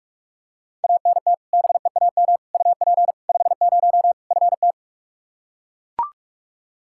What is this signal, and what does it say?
Morse code, heard as a keyed tone, about 42 wpm, 700 Hz. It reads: ANT BEAM UP 50 FT
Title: Practice QSO - Exchange 2 - Info - 45wpm